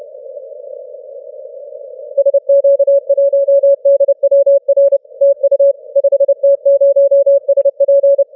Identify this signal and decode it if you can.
Morse code, heard as a keyed tone, about 29 wpm, 560 Hz. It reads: SQ1DWR TU 5T0SP
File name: wp / 5t0sp 12m cw 08 dec 2012 1143z